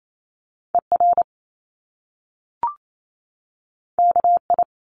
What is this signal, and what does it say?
Morse code, heard as a keyed tone, about 28 wpm, 700 Hz. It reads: ER KI
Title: Top Two Letter Combinations in Words 28wpm